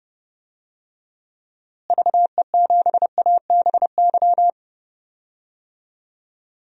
Morse code, heard as a keyed tone, about 31 wpm, 700 Hz. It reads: VE7ABY
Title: DXCC-100 Call Signs - 6 Characters - Repeated 2x - 30wpm